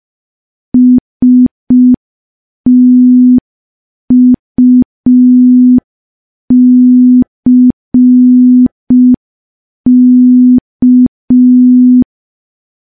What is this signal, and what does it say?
Morse code, heard as a keyed tone, about 5 wpm, 250 Hz. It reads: STUCK
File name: morse / morse-training-05WPM